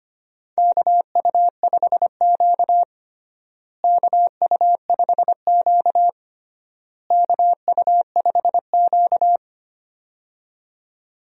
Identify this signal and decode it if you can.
Morse code, heard as a keyed tone, about 25 wpm, 700 Hz. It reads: KU5Q KU5Q KU5Q
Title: USA Call Signs - 4 Characters - Repeated 3x - 25wpm